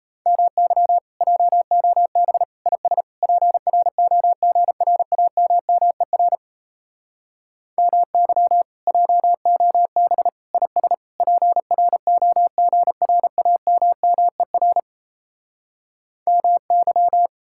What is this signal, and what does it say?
Morse code, heard as a keyed tone, about 37 wpm, 700 Hz. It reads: MY JOB IS PROGRAMMER MY JOB IS PROGRAMMER MY